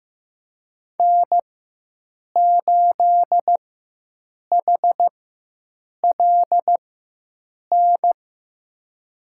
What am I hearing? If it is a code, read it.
Morse code, heard as a keyed tone, about 15 wpm, spaced as for 7 wpm, 700 Hz. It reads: N8HLN
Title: USA Call Signs - 5 Characters - 4x Character Spacing - 15wpm